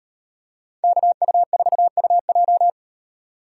Morse code, heard as a keyed tone, about 38 wpm, 700 Hz. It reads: KU4UJ